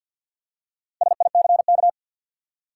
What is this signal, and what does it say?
Morse code, heard as a keyed tone, about 48 wpm, 700 Hz. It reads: SICK
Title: Four Letter Words - 50wpm